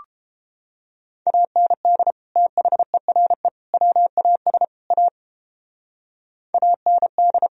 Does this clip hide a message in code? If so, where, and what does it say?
Yes: Morse code from 1.3 to 7.5 s, heard as a keyed tone, about 31 wpm, 700 Hz. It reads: AND THERE WAS A AND